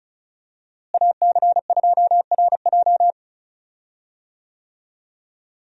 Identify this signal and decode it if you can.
Morse code, heard as a keyed tone, about 34 wpm, 700 Hz. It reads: AC2RJ
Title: USA Call Signs - 5 Characters - Repeated 3x - 35wpm